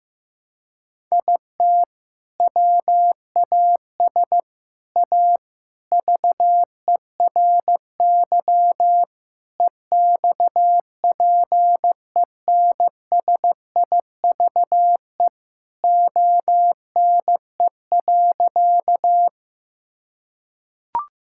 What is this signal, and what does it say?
Morse code, heard as a keyed tone, about 16 wpm, 700 Hz. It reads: IT WAS A VERY EXPENSIVE ONE.
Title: Sentences from Top 1000 Words - Review Part 6 - 15wpm